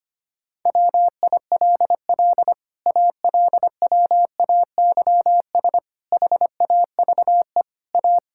Morse code, heard as a keyed tone, about 25 wpm, 700 Hz. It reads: WILL ALWAYS HAVE A
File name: Sets of 9 Words - 25wpm